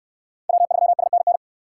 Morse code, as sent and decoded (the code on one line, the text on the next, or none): .-- ...-- ... -. -.
W3SNN